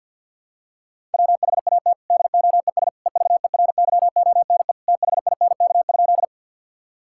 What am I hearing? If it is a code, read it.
Morse code, heard as a keyed tone, about 50 wpm, 700 Hz. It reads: WHAT DOES EVERYONE THINK?
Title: Sentences from Top 1000 Words - Review Part 5 - 50wpm